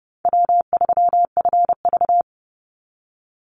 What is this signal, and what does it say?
Morse code, heard as a keyed tone, about 30 wpm, 700 Hz. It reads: W3FV
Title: Easy Call Signs - 30wpm